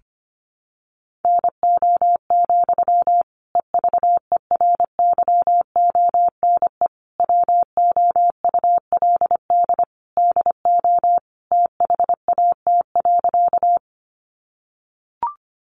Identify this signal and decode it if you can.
Morse code, heard as a keyed tone, about 25 wpm, 700 Hz. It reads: NO, EVERYONE WOULD DO THAT.
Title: Sentences from Top 800 Words 25wpm